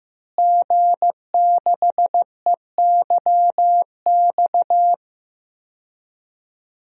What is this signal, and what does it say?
Morse code, heard as a keyed tone, about 15 wpm, 700 Hz. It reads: G6EYX
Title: DXCC-100 Call Signs - 5 Characters - Repeated 3x - 15wpm